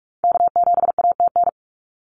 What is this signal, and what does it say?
Morse code, heard as a keyed tone, about 46 wpm, 700 Hz. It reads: K7ATN
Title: NA SOTA Activator Callsigns 45wpm